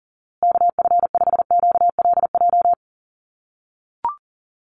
Morse code, heard as a keyed tone, about 40 wpm, 700 Hz. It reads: KF5QLJ
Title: USA Call Signs Repeated 3x 40wpm